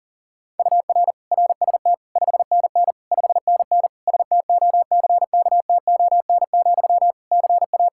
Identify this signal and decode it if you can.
Morse code, heard as a keyed tone, about 37 wpm, 700 Hz. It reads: UR RST 5NN 5NN STOCKTON, CA
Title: Practice QSO - Exchange 1 - Beginning - 40wpm